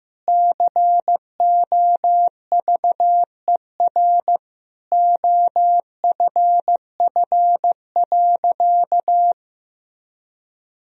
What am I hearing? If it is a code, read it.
Morse code, heard as a keyed tone, about 15 wpm, 700 Hz. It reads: COVER OFF.
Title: Sentences from Top 1000 Words - Review Part 1 - 15wpm